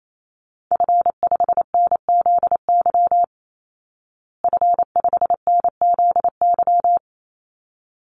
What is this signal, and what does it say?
Morse code, heard as a keyed tone, about 28 wpm, 690 Hz. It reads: F5NZY F5NZY
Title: Call Signs Repeated 3x - 28wpm